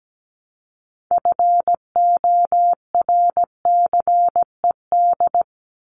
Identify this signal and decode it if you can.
Morse code, heard as a keyed tone, about 17 wpm, 700 Hz. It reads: FORCED
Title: ED Suffix - Top 100 Words with Base Words 17wpm